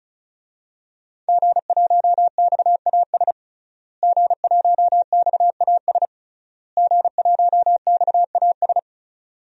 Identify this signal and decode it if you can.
Morse code, heard as a keyed tone, about 36 wpm, 700 Hz. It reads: G1XAS G1XAS G1XAS
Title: DXCC-100 Call Signs - 5 Characters - Repeated 3x - 35wpm